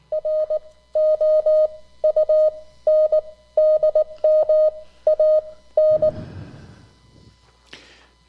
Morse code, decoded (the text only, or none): ROUNDMAN